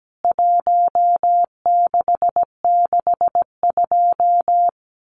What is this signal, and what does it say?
Morse code, heard as a keyed tone, about 17 wpm, 700 Hz. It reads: 1662
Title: CWT Exchange 17wpm